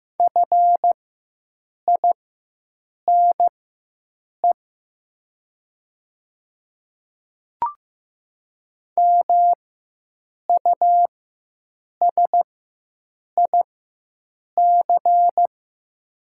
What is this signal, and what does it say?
Morse code, heard as a keyed tone, about 15 wpm, spaced as for 7 wpm, 700 Hz. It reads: FINE MUSIC
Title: 4x Character Spacing - Top 500 Words - 15wpm